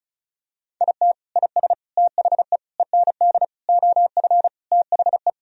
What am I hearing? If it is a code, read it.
Morse code, heard as a keyed tone, about 36 wpm, 700 Hz. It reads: IT IS THE END OF THE
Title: Sentences from Top 1000 Words - Review Part 1 - 35wpm